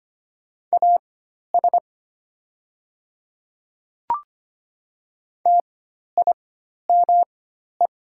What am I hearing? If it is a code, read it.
Morse code, heard as a keyed tone, about 25 wpm, spaced as for 12 wpm, 700 Hz. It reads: AS TIME
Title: 4x Character Spacing - Top 200 Words - 25wpm